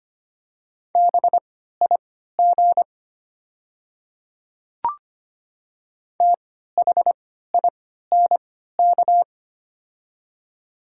Morse code, heard as a keyed tone, about 25 wpm, spaced as for 14 wpm, 700 Hz. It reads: BIG THINK
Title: 3x Character Spacing - Top 200 Words - 25wpm